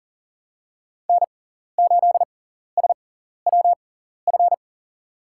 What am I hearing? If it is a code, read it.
Morse code, heard as a keyed tone, about 40 wpm, spaced as for 14 wpm, 700 Hz. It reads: N8SWF
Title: USA Call Signs - 5 Characters - 6x Character Spacing - 40wpm